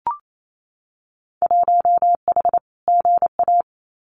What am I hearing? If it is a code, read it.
Morse code, heard as a keyed tone, about 28 wpm, 700 Hz. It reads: 1H GA
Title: Winter Field Day Exchanges Encore 28wpm